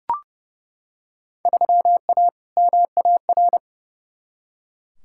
Morse code, heard as a keyed tone, about 30 wpm, 700 Hz. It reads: 3A MAR
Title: Field Day Exchanges Encore - 30wpm